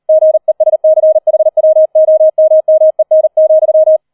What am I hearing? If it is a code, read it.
Morse code, heard as a keyed tone, about 38 wpm, 630 Hz. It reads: GESCHWOMMEN,